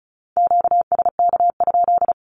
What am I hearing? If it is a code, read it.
Morse code, heard as a keyed tone, about 34 wpm, 700 Hz. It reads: QSK?
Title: Q-Codes 35wpm